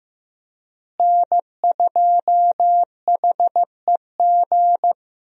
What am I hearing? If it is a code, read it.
Morse code, heard as a keyed tone, about 15 wpm, 700 Hz. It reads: N2HEG
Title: USA Call Signs - 5 Characters - Repeated 3x - 15wpm